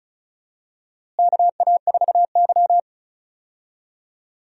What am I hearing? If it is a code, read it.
Morse code, heard as a keyed tone, about 36 wpm, 700 Hz. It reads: KA4Y
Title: USA Call Signs - 4 Characters - Repeated 2x - 35wpm